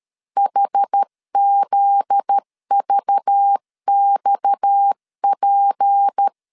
Morse code, heard as a keyed tone, about 13 wpm, 800 Hz. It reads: HZVXP
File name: LEZIONI / 7 GF9Z / 7 GF9Z 12wpm 2147